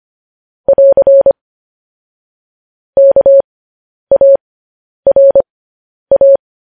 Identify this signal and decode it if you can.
Morse code, heard as a keyed tone, about 25 wpm, spaced as for 10 wpm, 550 Hz. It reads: <AR> KARA
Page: news headlines in morse code at 10 wpm